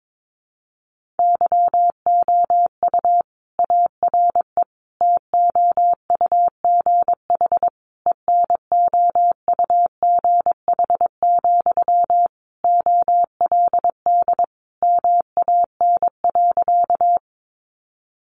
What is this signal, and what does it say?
Morse code, heard as a keyed tone, about 22 wpm, 700 Hz. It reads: YOU ARE TOUGH ENOUGH, OLD MAN.
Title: Sentences from Top 1200 Words 22wpm